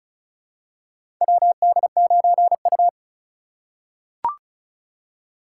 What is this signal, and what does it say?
Morse code, heard as a keyed tone, about 36 wpm, 690 Hz. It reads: WD9U
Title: USA Call Signs - 4 Characters - 3x Character Spacing - 35wpm